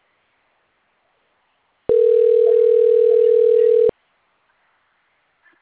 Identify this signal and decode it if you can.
Ringback tone (North America), 440 + 480 Hz.
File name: out- - - - - .97960